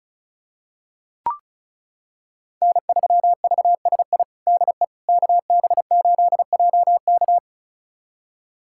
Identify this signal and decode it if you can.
Morse code, heard as a keyed tone, about 34 wpm, 700 Hz. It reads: N3VSI DE KB8JK
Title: USA Callsign DE Callsign - 5 Characters - 35wpm